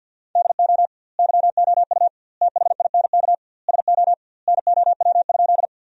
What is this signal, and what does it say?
Morse code, heard as a keyed tone, about 49 wpm, 700 Hz. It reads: DO YOU THINK SO NOW?
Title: 2x Word-Spacing - Sentences from Top 200 Words - 50wpm